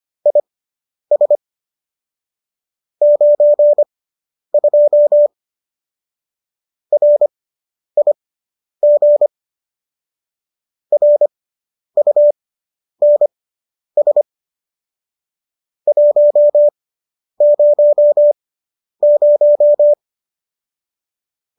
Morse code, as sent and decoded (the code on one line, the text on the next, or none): .. ... / ----. ..--- / .-. .. --. / .-. ..- -. ... / .---- ----- -----
IS 92 RIG RUNS 100